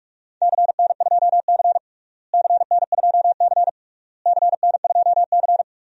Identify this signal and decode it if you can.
Morse code, heard as a keyed tone, about 46 wpm, 700 Hz. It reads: CN2C CN2C CN2C
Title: DXCC-100 Call Signs - 4 Characters - Repeated 3x - 45wpm